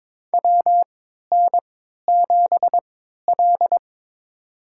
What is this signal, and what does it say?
Morse code, heard as a keyed tone, about 21 wpm, spaced as for 12 wpm, 700 Hz. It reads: WN7L
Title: USA Call Signs - 4 Characters - 3x Character Spacing - 22wpm